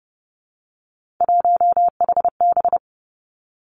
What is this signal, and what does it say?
Morse code, heard as a keyed tone, about 32 wpm, 700 Hz. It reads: 1HB
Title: Three Challenging Letter-Number Combinations 30wpm